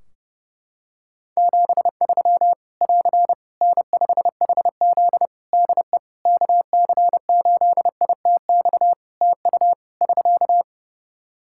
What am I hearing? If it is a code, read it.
Morse code, heard as a keyed tone, about 30 wpm, 700 Hz. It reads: 73 <AR> N5HZ DE KC8ITX TU <SK>
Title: Practice QSO - Exchange 4 - Ending - 30wpm